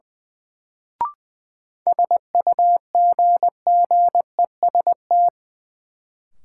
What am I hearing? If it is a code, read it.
Morse code, heard as a keyed tone, about 21 wpm, 700 Hz. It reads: SUGGEST
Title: Sentences from Top 1300 Words - New Word Drill Alt - 20wpm